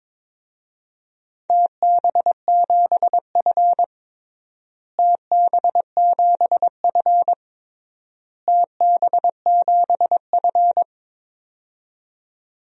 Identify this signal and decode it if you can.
Morse code, heard as a keyed tone, about 21 wpm, 700 Hz. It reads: TB7F TB7F TB7F
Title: DXCC-100 Call Signs - 4 Characters - Repeated 3x - 22wpm